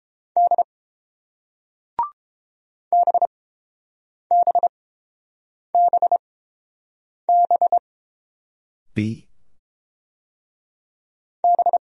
Morse code, heard as a keyed tone, about 29 wpm, spaced as for 7 wpm, 700 Hz. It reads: D BBBB B